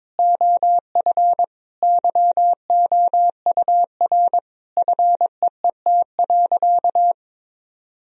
Morse code, heard as a keyed tone, about 22 wpm, 700 Hz. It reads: OF YOUR FEET.
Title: Sentences from Top 1000 Words - Review Part 1 - 22wpm